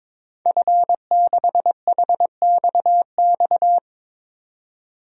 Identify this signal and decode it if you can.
Morse code, heard as a keyed tone, about 22 wpm, 700 Hz. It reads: F6HXX
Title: DXCC-100 Call Signs - 5 Characters - 22wpm